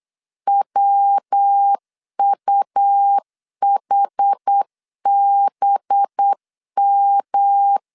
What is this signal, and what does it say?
Morse code, heard as a keyed tone, about 8 wpm, 800 Hz. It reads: WUHBM
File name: LEZIONI / 6 PKX0 / 6 PKX0 8wpm 3341